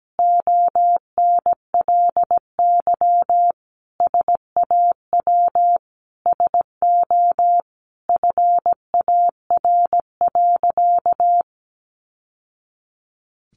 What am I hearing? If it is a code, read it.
Morse code, heard as a keyed tone, about 17 wpm, 700 Hz. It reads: ONLY SAW SO FAR.
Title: Sentences from Top 800 Words 17wpm